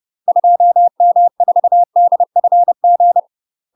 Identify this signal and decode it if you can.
Morse code, heard as a keyed tone, about 30 wpm, 700 Hz. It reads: 2M4DFG
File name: UK MASTER QSOs TRAINING / 30WPM / UK MASTER QSOs 0003